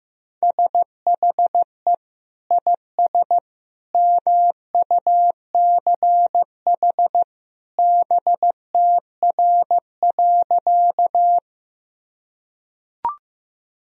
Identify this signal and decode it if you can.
Morse code, heard as a keyed tone, about 15 wpm, 700 Hz. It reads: SHE IS MUCH BTR.
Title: Sentences from Top 400 Words and Abbreviations 15wpm